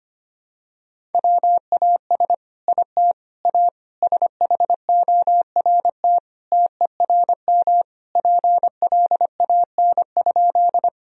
Morse code, heard as a keyed tone, about 25 wpm, 700 Hz. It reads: WAS IT A SHORT TERM PLAN?
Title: Sentences from Top 1000 Words - Review Part 4 - 25wpm